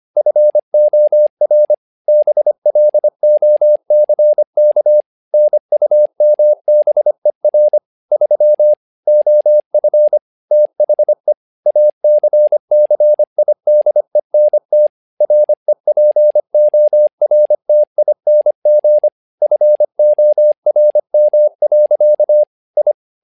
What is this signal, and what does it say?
Morse code, heard as a keyed tone, about 25 wpm, 600 Hz. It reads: FOR BLOCK NUMBER 3 OF THE ACCIDENT REPORTING FORM. I
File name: learning center / tower trouble / tower trouble 25wpm